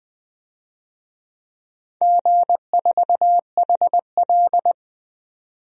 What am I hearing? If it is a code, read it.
Morse code, heard as a keyed tone, about 20 wpm, 700 Hz. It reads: G4HL
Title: DXCC-100 Call Signs - 4 Characters - 20wpm